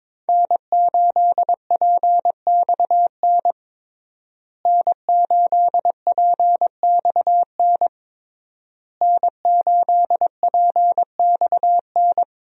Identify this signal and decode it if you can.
Morse code, heard as a keyed tone, about 22 wpm, 700 Hz. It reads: N8PXN N8PXN N8PXN
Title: USA Call Signs - 5 Characters - Repeated 3x - 22wpm